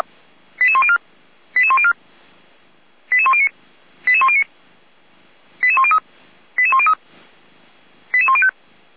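Five-tone selective call (ZVEI), heard as quick runs of five tones, tones of about 70 ms. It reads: 88104 88104 88109 88109 88103 88103 88105
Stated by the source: ZVEI five-tone call 88383, received on SDR.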